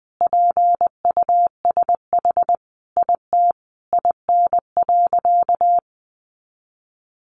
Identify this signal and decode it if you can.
Morse code, heard as a keyed tone, about 20 wpm, 700 Hz. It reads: PUSH IT IN.
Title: Sentences from Top 1300 Words 20wpm